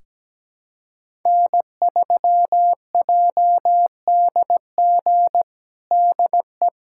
Morse code, heard as a keyed tone, about 17 wpm, 700 Hz. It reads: N3JDG DE
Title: USA Callsign DE Callsign 17wpm